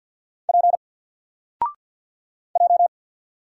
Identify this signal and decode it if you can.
Morse code, heard as a keyed tone, about 48 wpm, 700 Hz. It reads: P J
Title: Single Letter-Number Alt 50wpm